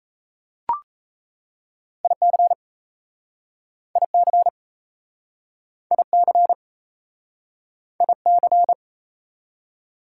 Morse code, heard as a keyed tone, about 37 wpm, 700 Hz. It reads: IC IC IC IC